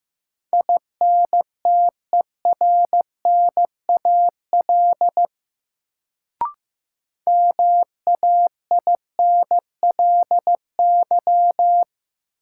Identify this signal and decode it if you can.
Morse code, heard as a keyed tone, about 15 wpm, 700 Hz. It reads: INTERNAL MAINLY
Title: Sentences from Top 1400 Words - New Word Drill Alt - 15wpm